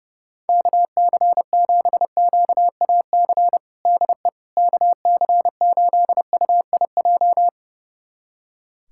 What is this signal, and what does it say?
Morse code, heard as a keyed tone, about 30 wpm, 700 Hz. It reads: KC7QAC DE KC8UIJ
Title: USA Callsign DE Callsign - 6 Characters - 30wpm